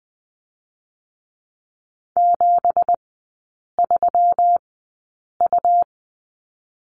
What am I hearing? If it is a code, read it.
Morse code, heard as a keyed tone, about 20 wpm, spaced as for 8 wpm, 700 Hz. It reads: 73U
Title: 166 Lesson 27e - Number 7 - Three Characters 20wpm